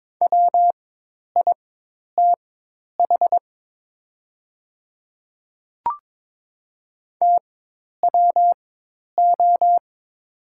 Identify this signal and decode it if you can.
Morse code, heard as a keyed tone, about 23 wpm, spaced as for 10 wpm, 700 Hz. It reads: WITH TWO
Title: 4x Character Spacing - Top 100 Words - 22wpm